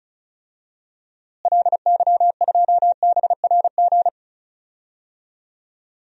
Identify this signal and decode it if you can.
Morse code, heard as a keyed tone, about 36 wpm, 700 Hz. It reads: LY2BRG
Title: DXCC-100 Call Signs - 6 Characters - Repeated 3x - 35wpm